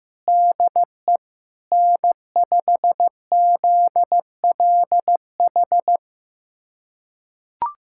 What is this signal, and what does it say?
Morse code, heard as a keyed tone, about 15 wpm, 700 Hz. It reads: DE N5ZLH
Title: USA Callsign DE Callsign - 5 Characters - 15wpm